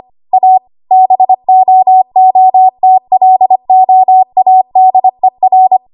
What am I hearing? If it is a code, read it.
Morse code, heard as a keyed tone, about 25 wpm, 750 Hz. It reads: A BOOTLOADER
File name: Archive / 25 WPM / 25WPM